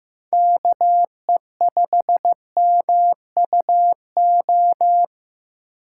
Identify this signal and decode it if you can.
Morse code, heard as a keyed tone, about 15 wpm, 700 Hz. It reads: KE5MUO
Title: USA Call Signs - 6 Characters - 2x Character Spacing - 15wpm